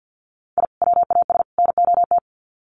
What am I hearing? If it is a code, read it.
Morse code, heard as a keyed tone, about 50 wpm, 700 Hz. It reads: I WAS NOT
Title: Sets of 3 Words 50wpm